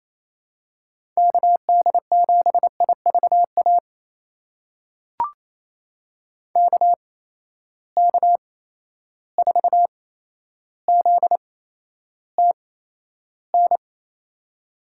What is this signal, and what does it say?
Morse code, heard as a keyed tone, about 28 wpm, spaced as for 12 wpm, 700 Hz. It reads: KD7IVA KK4ZTN
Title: USA Call Signs - 6 Characters - 8x Character Spacing - 28wpm